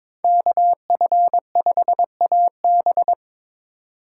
Morse code, heard as a keyed tone, about 22 wpm, 700 Hz. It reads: KF5AB
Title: USA Call Signs - 2x Character Spacing - 22wpm